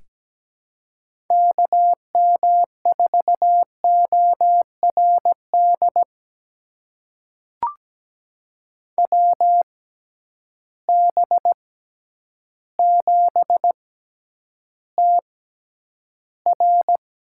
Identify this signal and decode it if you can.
Morse code, heard as a keyed tone, about 17 wpm, spaced as for 9 wpm, 700 Hz. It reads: KM4ORD WB7TR
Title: USA Call Signs - 6 Characters - 6x Character Spacing - 17wpm